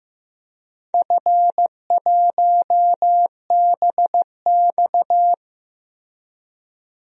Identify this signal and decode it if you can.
Morse code, heard as a keyed tone, about 15 wpm, 700 Hz. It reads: F1BX
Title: DXCC-100 Call Signs - 4 Characters - Repeated 2x - 15wpm